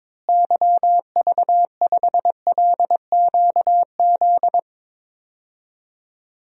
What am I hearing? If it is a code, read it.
Morse code, heard as a keyed tone, about 21 wpm, 700 Hz. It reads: YV5LQZ
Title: DXCC-100 Call Signs - 6 Characters - Repeated 2x - 22wpm